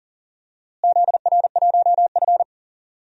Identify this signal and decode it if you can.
Morse code, heard as a keyed tone, about 40 wpm, 700 Hz. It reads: ZR1F
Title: DXCC-100 Call Signs - 4 Characters - Repeated 2x - 40wpm